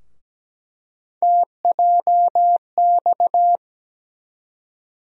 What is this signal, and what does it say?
Morse code, heard as a keyed tone, about 17 wpm, 700 Hz. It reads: TJX
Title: USA Call Sign Postfixes - 3 Letters 17wpm